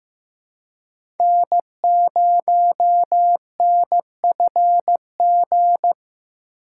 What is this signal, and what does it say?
Morse code, heard as a keyed tone, about 15 wpm, 700 Hz. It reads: N0NFG
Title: USA Call Signs - 5 Characters - 3x Character Spacing - 15wpm